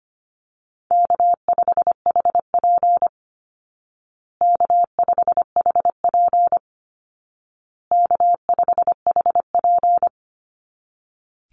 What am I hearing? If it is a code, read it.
Morse code, heard as a keyed tone, about 25 wpm, 700 Hz. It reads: K5HP K5HP K5HP
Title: Easy Call Signs Repeated 3x - 25wpm